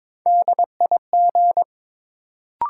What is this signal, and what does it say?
Morse code, heard as a keyed tone, about 22 wpm, 700 Hz. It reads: DIG
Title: Sentences from Top 1600 Words - New Word Drill - 22wpm